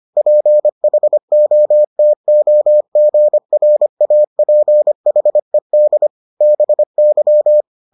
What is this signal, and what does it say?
Morse code, heard as a keyed tone, about 25 wpm, 600 Hz. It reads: PHOTOGRAPHED BY